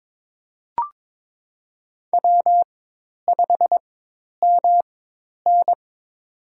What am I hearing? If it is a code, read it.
Morse code, heard as a keyed tone, about 22 wpm, spaced as for 10 wpm, 700 Hz. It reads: W5MN
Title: USA Call Signs - 4 Characters - 4x Character Spacing - 22wpm